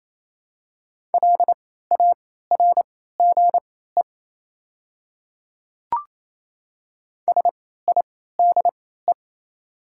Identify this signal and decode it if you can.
Morse code, heard as a keyed tone, about 28 wpm, spaced as for 16 wpm, 700 Hz. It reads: LARGE SIDE
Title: 3x Character Spacing - Top 200 Words - 28wpm